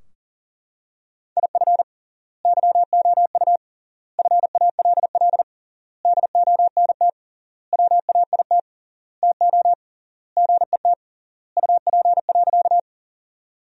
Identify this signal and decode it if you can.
Morse code, heard as a keyed tone, about 40 wpm, 700 Hz. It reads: IF YOU FALL DONT WAIT TO GET UP.
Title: 3x Word-Spacing - Sentences from Top 500 Words - 40wpm